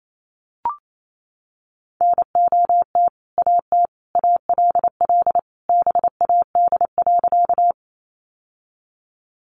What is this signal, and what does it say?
Morse code, heard as a keyed tone, about 28 wpm, 700 Hz. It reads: NOT AT ALL BAD.
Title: Sentences from Top 700 Words 28wpm